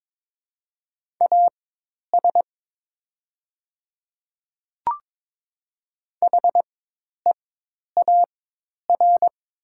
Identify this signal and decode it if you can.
Morse code, heard as a keyed tone, about 22 wpm, spaced as for 10 wpm, 700 Hz. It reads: AS HEAR
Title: 4x Character Spacing - Top 500 Words - 22wpm